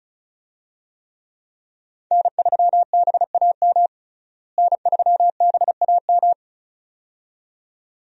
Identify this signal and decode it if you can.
Morse code, heard as a keyed tone, about 36 wpm, 700 Hz. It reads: N3BAM N3BAM
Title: USA Call Signs - 5 Characters - Repeated 2x - 35wpm